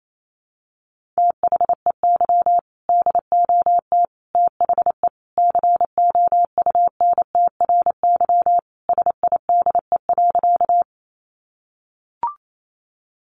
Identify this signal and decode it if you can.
Morse code, heard as a keyed tone, about 28 wpm, 700 Hz. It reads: THEY DOT THE COUNTRY SIDE.
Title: Sentences from Top 1500 Words 28wpm